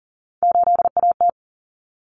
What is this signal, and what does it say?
Morse code, heard as a keyed tone, about 40 wpm, 700 Hz. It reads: 8AT
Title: Common Cut Numbers - 3-Digit 40wpm